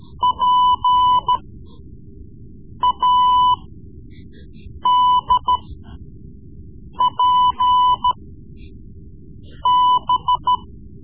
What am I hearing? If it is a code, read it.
Morse code, heard as a keyed tone, about 14 wpm, spaced as for 5 wpm, 1000 Hz. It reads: PADPB
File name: Morse Lessons / Morse Lessons / Tape-3B